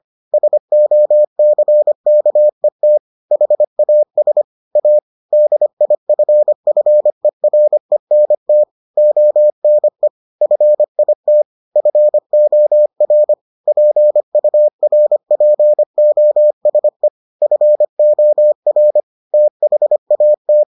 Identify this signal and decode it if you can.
Morse code, heard as a keyed tone, about 25 wpm, 600 Hz. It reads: SOCKET HAS A DIFFERENT ONE FIT FOR PURPOSE FOR THAT